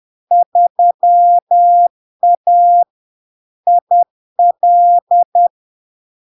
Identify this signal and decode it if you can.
Morse code, heard as a keyed tone, about 10 wpm, 700 Hz. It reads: 3A IL
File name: LARK-FD-2022-Log-in-CW-10wpm-700-Hz